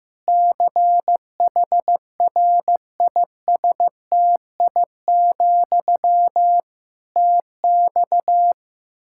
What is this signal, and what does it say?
Morse code, heard as a keyed tone, about 15 wpm, 700 Hz. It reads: CHRISTI, TX
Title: Practice QSO - Exchange 1 - Beginning - 15wpm